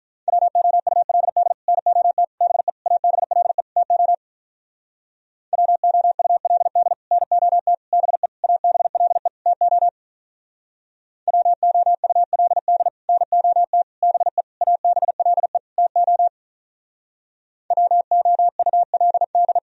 Morse code, heard as a keyed tone, about 47 wpm, 700 Hz. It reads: WOULD NOT BE ABLE TO WOULD NOT BE ABLE TO WOULD NOT BE ABLE TO WOULD